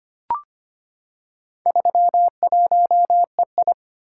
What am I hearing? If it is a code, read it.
Morse code, heard as a keyed tone, about 25 wpm, 700 Hz. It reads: 31EI
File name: practice / 25wpm / Four Letter-Number Combinations - 25wpm